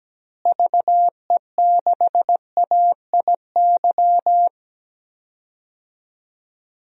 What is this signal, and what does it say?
Morse code, heard as a keyed tone, about 17 wpm, 700 Hz. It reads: VE6AIY
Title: DXCC-100 Call Signs - 6 Characters - Repeated 3x - 17wpm